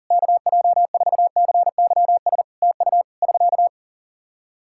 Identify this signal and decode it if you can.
Morse code, heard as a keyed tone, about 40 wpm, 700 Hz. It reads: KJ4CYS TU <SK>
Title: Practice QSO - Exchange 4 - Ending - 40wpm